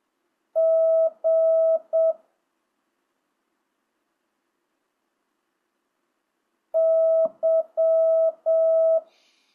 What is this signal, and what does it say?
Morse code, heard as a keyed tone, about 7 wpm, 660 Hz. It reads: G Y
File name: jh8cbh / 2025cw